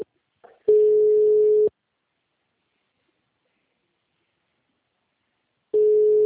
Ringback tone (Europe, 425 Hz).